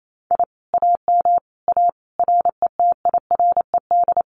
Morse code, heard as a keyed tone, about 28 wpm, 700 Hz. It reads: I AM A RETIRED
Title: Common QSO Phrases Encore 28wpm